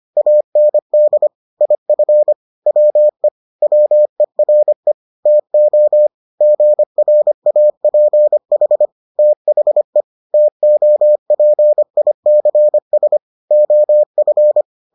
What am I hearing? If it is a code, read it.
Morse code, heard as a keyed tone, about 25 wpm, 600 Hz. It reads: AND IF WE WERE TO GRAPH THE TOPICS OF